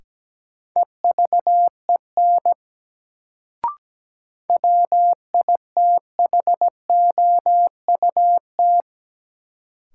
Morse code, heard as a keyed tone, about 17 wpm, 700 Hz. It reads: EVEN WITHOUT
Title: Sentences from Top 600 Words - New Word Drill - 17wpm